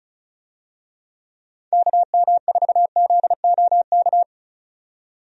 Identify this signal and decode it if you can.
Morse code, heard as a keyed tone, about 34 wpm, 700 Hz. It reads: KM4ZOK
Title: USA Call Signs - 6 Characters - 35wpm